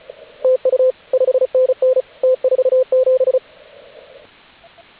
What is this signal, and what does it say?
Morse code, heard as a keyed tone, about 36 wpm, 520 Hz, with radio noise. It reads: TU 5NN T47
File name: AUDIO2019 / HG8DX CW